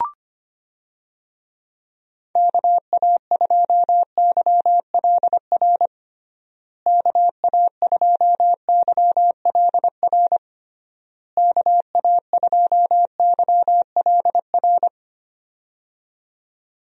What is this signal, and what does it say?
Morse code, heard as a keyed tone, about 25 wpm, 700 Hz. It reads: KA2YLR KA2YLR KA2YLR
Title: USA Call Signs - 6 Characters - Repeated 3x - 25wpm